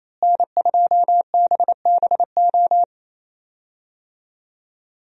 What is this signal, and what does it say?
Morse code, heard as a keyed tone, about 27 wpm, 700 Hz. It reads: N2BBO
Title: USA Call Signs - 5 Characters - Repeated 2x - 28wpm